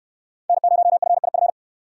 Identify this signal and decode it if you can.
Morse code, heard as a keyed tone, about 65 wpm, 700 Hz. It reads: N0FIF